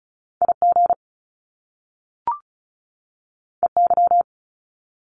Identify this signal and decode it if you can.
Morse code, heard as a keyed tone, about 36 wpm, 700 Hz. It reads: IG EY
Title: Top Two Letter Combinations in Words 35wpm